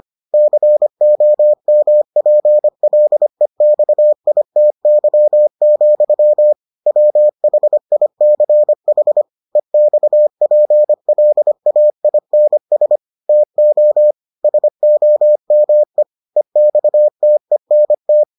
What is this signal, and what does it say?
Morse code, heard as a keyed tone, about 25 wpm, 600 Hz. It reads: COMPLEXITY, WHICH EXPLAINS TO SOME EXTENT